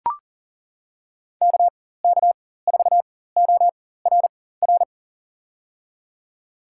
Morse code, heard as a keyed tone, about 40 wpm, spaced as for 19 wpm, 700 Hz. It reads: KK4ORR
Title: USA Call Signs - 6 Characters - 4x Character Spacing - 40wpm